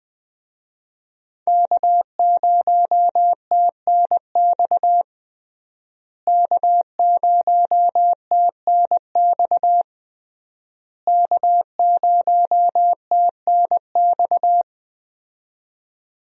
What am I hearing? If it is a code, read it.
Morse code, heard as a keyed tone, about 20 wpm, 700 Hz. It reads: K0TNX K0TNX K0TNX
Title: USA Call Signs - 5 Characters - Repeated 3x - 20wpm